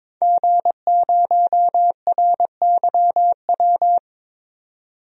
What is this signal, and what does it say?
Morse code, heard as a keyed tone, about 22 wpm, 700 Hz. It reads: G0RYW
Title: DXCC-100 Call Signs - 5 Characters - 22wpm